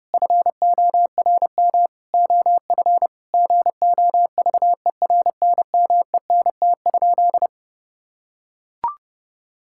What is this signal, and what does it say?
Morse code, heard as a keyed tone, about 30 wpm, 700 Hz. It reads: FORM OF GOVERNMENT?
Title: Sentences from Top 1000 Words - Review Part 6 - 30wpm